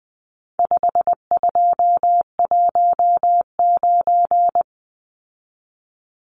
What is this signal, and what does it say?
Morse code, heard as a keyed tone, about 20 wpm, 700 Hz. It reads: 5219
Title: Numbers - 4-Digit 20wpm